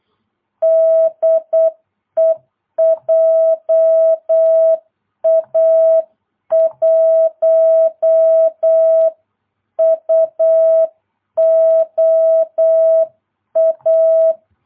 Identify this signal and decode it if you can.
Morse code, heard as a keyed tone, about 8 wpm, 650 Hz. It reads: DE JA1UOA